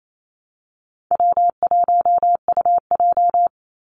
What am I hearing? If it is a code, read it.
Morse code, heard as a keyed tone, about 28 wpm, 700 Hz. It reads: W1UJ
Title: Easy Call Signs - 28wpm